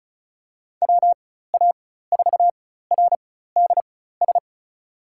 Morse code, heard as a keyed tone, about 36 wpm, spaced as for 16 wpm, 700 Hz. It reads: WA4RDS
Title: USA Call Signs - 6 Characters - 4x Character Spacing - 35wpm